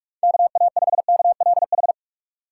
Morse code, heard as a keyed tone, about 43 wpm, 700 Hz. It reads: KA5KLH
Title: USA Call Signs - 6 Characters - 6x Character Spacing - 45wpm